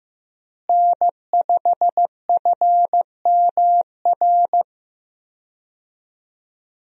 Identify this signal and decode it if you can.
Morse code, heard as a keyed tone, about 15 wpm, 700 Hz. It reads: N5FMR
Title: USA Call Signs - 5 Characters - Repeated 3x - 15wpm